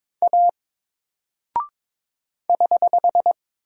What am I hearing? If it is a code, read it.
Morse code, heard as a keyed tone, about 22 wpm, 700 Hz. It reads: A <HH>
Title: Single Character (Letter:Number:Punctuation:Prosign) 22wpm